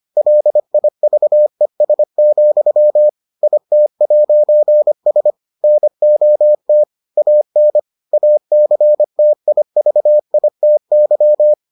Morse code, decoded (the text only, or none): LIVES, IT'S NOT AN ACTIVITY